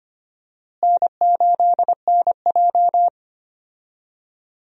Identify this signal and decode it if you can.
Morse code, heard as a keyed tone, about 25 wpm, 700 Hz. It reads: N8NJ
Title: USA Call Signs - 4 Characters - Repeated 2x - 25wpm